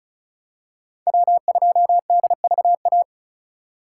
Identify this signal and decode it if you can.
Morse code, heard as a keyed tone, about 36 wpm, 700 Hz. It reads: W2DVA
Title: USA Call Signs - 5 Characters - 3x Character Spacing - 35wpm